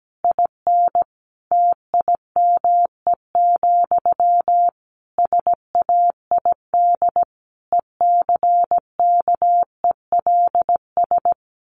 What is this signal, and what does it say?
Morse code, heard as a keyed tone, about 17 wpm, 700 Hz. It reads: IN TIME, SAID ECKELS
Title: A Sound Of Thunder 17wpm